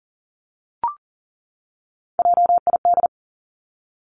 Morse code, heard as a keyed tone, about 40 wpm, 700 Hz. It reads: JID
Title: Three Challenging Letter-Number Combinations 40wpm